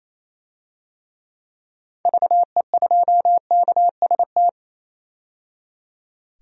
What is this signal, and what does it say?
Morse code, heard as a keyed tone, about 28 wpm, 700 Hz. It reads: VE2KST
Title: DXCC-100 Call Signs - 6 Characters - 28wpm